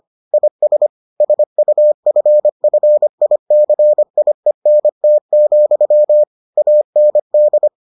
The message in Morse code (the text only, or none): IS SUFFICIENT, AND